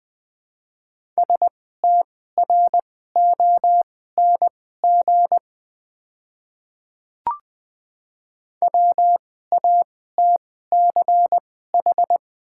Morse code, heard as a keyed tone, about 20 wpm, spaced as for 14 wpm, 700 Hz. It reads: STRONG WATCH
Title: 2x Character Spacing - Top 500 Words - 20wpm